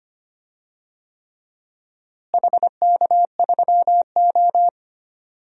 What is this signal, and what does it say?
Morse code, heard as a keyed tone, about 25 wpm, 700 Hz. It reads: HK3O
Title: DXCC-100 Call Signs - 4 Characters - Repeated 3x - 25wpm